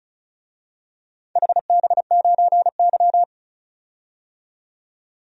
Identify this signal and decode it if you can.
Morse code, heard as a keyed tone, about 36 wpm, 700 Hz. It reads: HB9Y
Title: DXCC-100 Call Signs - 4 Characters - 35wpm